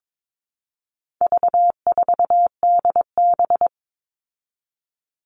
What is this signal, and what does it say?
Morse code, heard as a keyed tone, about 22 wpm, 700 Hz. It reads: V4DB
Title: All Cut Numbers - 4-Digit 22wpm